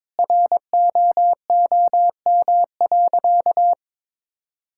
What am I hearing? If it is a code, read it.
Morse code, heard as a keyed tone, about 21 wpm, 700 Hz. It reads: ROOM.
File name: practice / 22wpm / Sentences with 4-Word Phrases - 22wpm